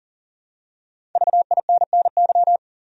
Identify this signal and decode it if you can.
Morse code, heard as a keyed tone, about 38 wpm, 700 Hz. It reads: VINNY
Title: USA Common Names - Max 5 Letters - 40wpm